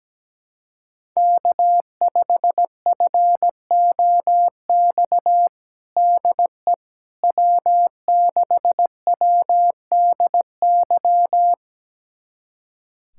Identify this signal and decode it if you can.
Morse code, heard as a keyed tone, about 17 wpm, 700 Hz. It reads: K5FOX DE W6WDY
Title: USA Callsign DE Callsign - 5 Characters - 17wpm